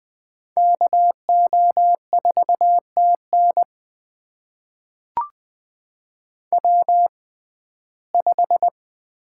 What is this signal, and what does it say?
Morse code, heard as a keyed tone, about 20 wpm, spaced as for 14 wpm, 700 Hz. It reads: KO4TN W5
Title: USA Call Signs - 5 Characters - 6x Character Spacing - 20wpm